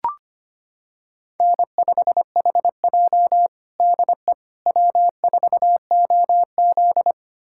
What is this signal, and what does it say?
Morse code, heard as a keyed tone, about 25 wpm, 700 Hz. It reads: N5HJ DE W4OZ
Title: USA Callsign DE Callsign - 4 Characters - 25wpm